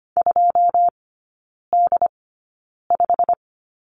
Morse code, heard as a keyed tone, about 25 wpm, spaced as for 9 wpm, 700 Hz. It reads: 2D5
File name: course / 25:10wpm / 093 Lesson 16e - Letter c - Three Characters 25wpm